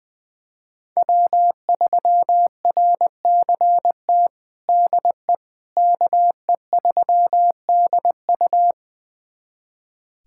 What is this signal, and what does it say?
Morse code, heard as a keyed tone, about 20 wpm, 700 Hz. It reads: W3RCT DE KE3DU
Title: USA Callsign DE Callsign - 5 Characters - 20wpm